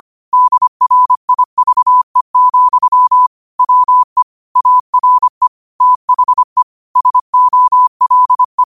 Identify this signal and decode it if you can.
Morse code, heard as a keyed tone, about 25 wpm, 1000 Hz. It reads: DRIVE, WE ARE THE SOLE
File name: qotdfiles / 25wpm